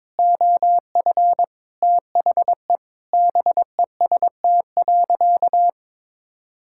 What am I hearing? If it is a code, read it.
Morse code, heard as a keyed tone, about 22 wpm, 700 Hz. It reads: OF THE BEST.
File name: practice / 22wpm / Sentences with 4-Word Phrases - 22wpm